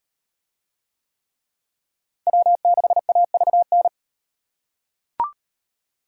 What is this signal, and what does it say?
Morse code, heard as a keyed tone, about 37 wpm, 700 Hz. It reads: W6AVN